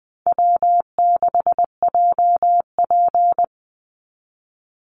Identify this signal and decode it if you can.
Morse code, heard as a keyed tone, about 20 wpm, 700 Hz. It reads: W6JP
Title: NA SOTA Chaser Callsigns 20wpm